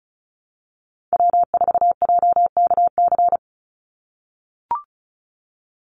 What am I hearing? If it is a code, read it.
Morse code, heard as a keyed tone, about 36 wpm, 700 Hz. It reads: W4JKC
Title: NA SOTA Chaser Callsigns 35wpm